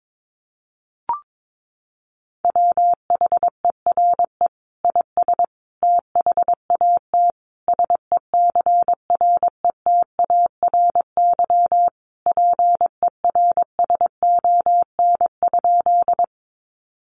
Morse code, heard as a keyed tone, about 22 wpm, 700 Hz. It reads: WHERE IS THAT SECRETARY PERSON?
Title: Sentences from Top 1300 Words 22wpm